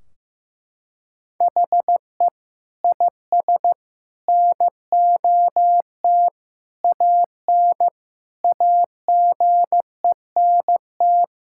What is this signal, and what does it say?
Morse code, heard as a keyed tone, about 14 wpm, 700 Hz. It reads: HE IS NOT AN AGENT
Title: Sentences from Top 1600 Words 15wpm